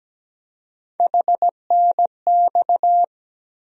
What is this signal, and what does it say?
Morse code, heard as a keyed tone, about 17 wpm, 700 Hz. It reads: HNX
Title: USA Call Sign Postfixes - 3 Letters 17wpm